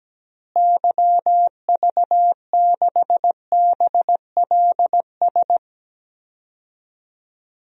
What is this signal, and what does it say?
Morse code, heard as a keyed tone, about 17 wpm, 700 Hz. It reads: YV6BLS
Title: DXCC-100 Call Signs - 6 Characters - Repeated 2x - 17wpm